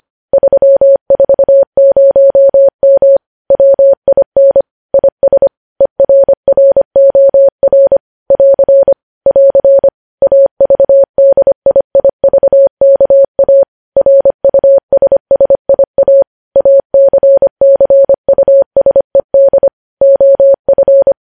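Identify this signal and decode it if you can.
Morse code, heard as a keyed tone, about 25 wpm, 550 Hz. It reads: 340M WIN IS ERROR <AR> <AR> AVDIIVKA RUSSIA ACCUSED OF
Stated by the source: Morse code news headlines sent at 25 wpm.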